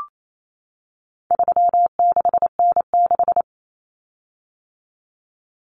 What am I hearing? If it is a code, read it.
Morse code, heard as a keyed tone, about 28 wpm, 700 Hz. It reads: 36N6
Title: Common Cut Numbers - 4-Digit 28wpm